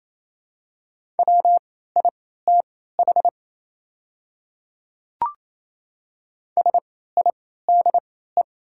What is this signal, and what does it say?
Morse code, heard as a keyed tone, about 28 wpm, spaced as for 16 wpm, 700 Hz. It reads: WITH SIDE
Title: 3x Character Spacing - Top 200 Words - 28wpm